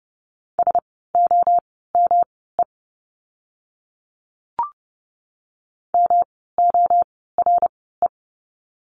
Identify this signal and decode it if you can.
Morse code, heard as a keyed tone, about 30 wpm, spaced as for 17 wpm, 700 Hz. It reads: SOME MORE
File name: course / 30:10wpm / 237 Lesson 38f - Letter q - Words 30wpm